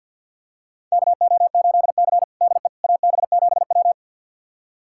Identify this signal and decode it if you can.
Morse code, heard as a keyed tone, about 47 wpm, 700 Hz. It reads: KO8C DE AB7W
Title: USA Callsign DE Callsign - 4 Characters - 50wpm